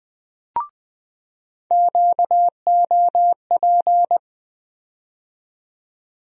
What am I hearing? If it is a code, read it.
Morse code, heard as a keyed tone, about 20 wpm, 700 Hz. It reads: QOP
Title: USA Call Sign Postfixes - 3 Letters 20wpm